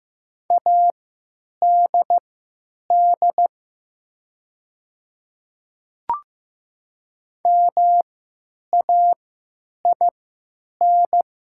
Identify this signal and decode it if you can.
Morse code, heard as a keyed tone, about 15 wpm, spaced as for 9 wpm, 700 Hz. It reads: ADD MAIN
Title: 3x Character Spacing - Top 300 Words - 15wpm